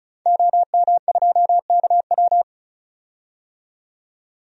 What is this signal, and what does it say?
Morse code, heard as a keyed tone, about 36 wpm, 700 Hz. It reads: OM2KW
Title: DXCC-100 Call Signs - 5 Characters - 35wpm